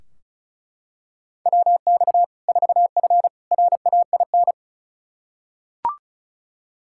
Morse code, heard as a keyed tone, about 36 wpm, 700 Hz. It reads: WX 4F RAIN
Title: Practice QSO - Exchange 2 - Info - 35wpm